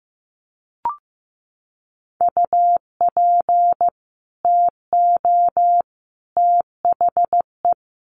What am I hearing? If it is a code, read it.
Morse code, heard as a keyed tone, about 15 wpm, 700 Hz. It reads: UP TO THE
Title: Sets of 3 Words 15wpm